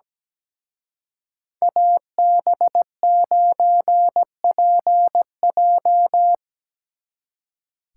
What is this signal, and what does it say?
Morse code, heard as a keyed tone, about 20 wpm, 700 Hz. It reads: AB9PJ
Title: Speed Racing - USA 2x2 Call Signs 17wpm